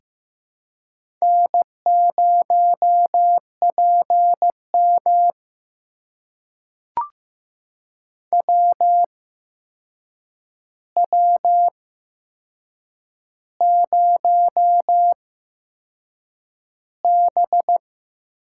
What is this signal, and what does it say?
Morse code, heard as a keyed tone, about 15 wpm, spaced as for 6 wpm, 700 Hz. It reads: N0PM WW0B
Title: USA Call Signs - 4 Characters - 8x Character Spacing - 15wpm